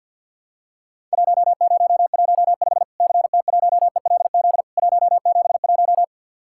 Morse code, heard as a keyed tone, about 50 wpm, 700 Hz. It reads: 1015 CT1ELZ 171
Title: Worked All Europe DX Contest - QTC Exchange with Repeats - 50wpm